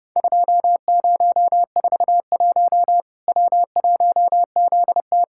Morse code, heard as a keyed tone, about 30 wpm, 700 Hz. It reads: 2041 W1ZT
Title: Worked All Europe DX Contest - QTC Exchange - 30wpm